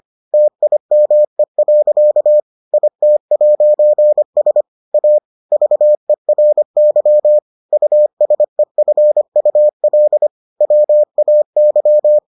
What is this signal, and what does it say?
Morse code, heard as a keyed tone, about 25 wpm, 600 Hz. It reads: TIME. IT'S A VERY USEFUL WAY